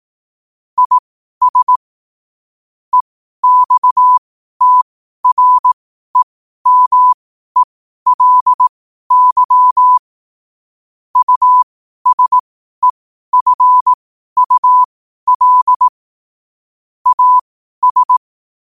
Morse code, heard as a keyed tone, about 18 wpm, spaced as for 12 wpm, 1000 Hz. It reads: IS EXTREMELY USEFUL AS
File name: qotdfiles / 12wpm